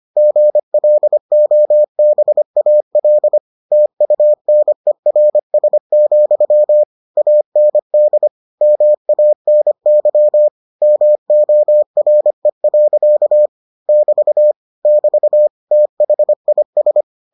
Morse code, heard as a keyed tone, about 25 wpm, 600 Hz. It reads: GLOBAL TUNERS, AND MANY MORE. <BT> <BT> THIS